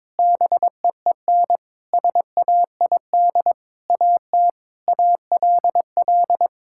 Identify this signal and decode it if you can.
Morse code, heard as a keyed tone, about 22 wpm, 700 Hz. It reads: BEEN SAID AT ALL
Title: Sentences from Top 1000 Words - Review Part 5 - 22wpm